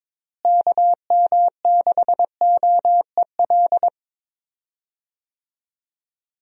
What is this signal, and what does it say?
Morse code, heard as a keyed tone, about 22 wpm, 700 Hz. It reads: KM6OEL
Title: USA Call Signs - 6 Characters - Repeated 3x - 22wpm